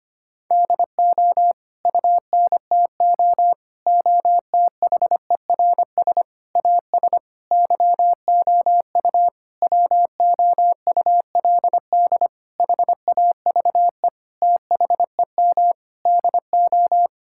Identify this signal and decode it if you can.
Morse code, heard as a keyed tone, about 25 wpm, 700 Hz. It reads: DO UNTO OTHERS AS YOU WOULD HAVE THEM DO
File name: practice / 25wpm / Sentences with 9-Word Phrases - 25wpm